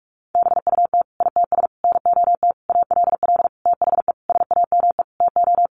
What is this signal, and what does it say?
Morse code, heard as a keyed tone, about 44 wpm, 700 Hz. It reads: BUT ITS NOT ALL THE SAME TO
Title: Sentences from Top 600 Words 45wpm